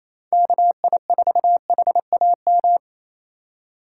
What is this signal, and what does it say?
Morse code, heard as a keyed tone, about 28 wpm, 700 Hz. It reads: KI4HAM
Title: USA Call Signs - 6 Characters - 3x Character Spacing - 28wpm